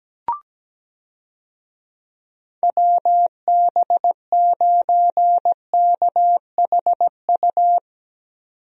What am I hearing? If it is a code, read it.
Morse code, heard as a keyed tone, about 17 wpm, 700 Hz. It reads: WB9KHU
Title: USA Call Signs - 6 Characters - 17wpm